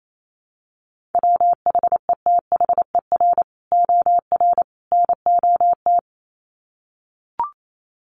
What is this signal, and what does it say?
Morse code, heard as a keyed tone, about 28 wpm, 700 Hz. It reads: WHETHER OR NOT
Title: Sets of 3 Words 28wpm